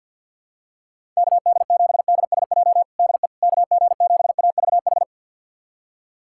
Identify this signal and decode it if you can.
Morse code, heard as a keyed tone, about 47 wpm, 700 Hz. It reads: KD7DSJ DE KG7AVH
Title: USA Callsign DE Callsign 50wpm